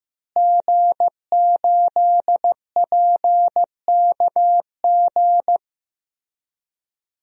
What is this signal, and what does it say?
Morse code, heard as a keyed tone, about 16 wpm, 700 Hz. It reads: G8PKG
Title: DXCC-100 Call Signs - 5 Characters - Repeated 3x - 15wpm